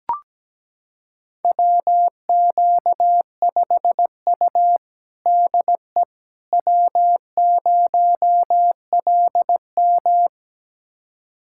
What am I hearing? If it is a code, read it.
Morse code, heard as a keyed tone, about 17 wpm, 700 Hz. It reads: WQ5U DE W0LM
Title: USA Callsign DE Callsign - 4 Characters - 17wpm